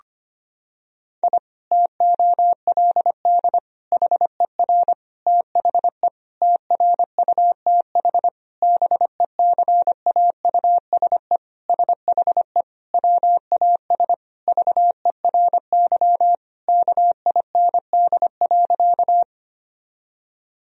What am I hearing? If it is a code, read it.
Morse code, heard as a keyed tone, about 25 wpm, 700 Hz. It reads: I TOLD HER THE TRUTH BECAUSE SHE WAS VERY KIND.
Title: Sentences from Top 1000 Words - Review Part 3 - 25wpm